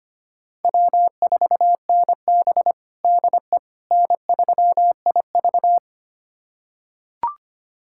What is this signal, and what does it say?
Morse code, heard as a keyed tone, about 25 wpm, 700 Hz. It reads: W4NB DE N3IV
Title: USA Callsign DE Callsign - 4 Characters - 25wpm